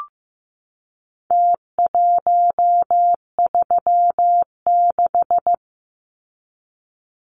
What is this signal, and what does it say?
Morse code, heard as a keyed tone, about 15 wpm, 700 Hz. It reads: T136
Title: Numbers - Contest Style - 4-Digit 15wpm